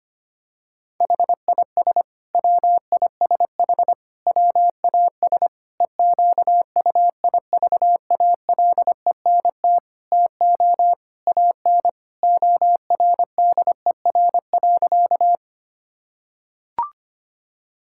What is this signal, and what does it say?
Morse code, heard as a keyed tone, about 25 wpm, 700 Hz. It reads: HIS WISH WAS EQUIVALENT TO AN ORDER.
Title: Sentences from Top 1600 Words 25wpm